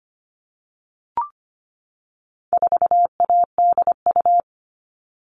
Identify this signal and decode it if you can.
Morse code, heard as a keyed tone, about 25 wpm, 700 Hz. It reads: 4ADU
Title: All Cut Numbers - 4-Digit 25wpm